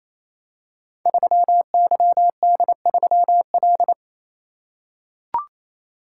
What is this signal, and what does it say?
Morse code, heard as a keyed tone, about 29 wpm, 700 Hz. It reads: 3YD3L
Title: Code Groups - 5 Letter:Numbers - 28wpm